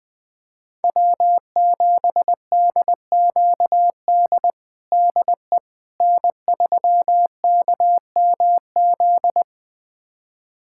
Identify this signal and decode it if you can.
Morse code, heard as a keyed tone, about 21 wpm, 700 Hz. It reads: W7DQD DE N3KMZ
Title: USA Callsign DE Callsign - 5 Characters - 20wpm